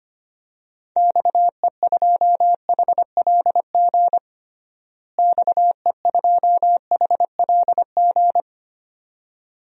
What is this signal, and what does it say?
Morse code, heard as a keyed tone, about 25 wpm, 700 Hz. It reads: XE2HLG XE2HLG
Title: DXCC-100 Call Signs - 6 Characters - Repeated 2x - 25wpm